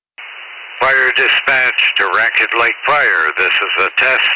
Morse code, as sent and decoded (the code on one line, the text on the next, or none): none